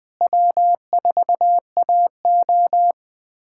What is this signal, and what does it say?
Morse code, heard as a keyed tone, about 20 wpm, 700 Hz. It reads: W4AO
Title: USA Callsign DE Callsign - 4 Characters - 20wpm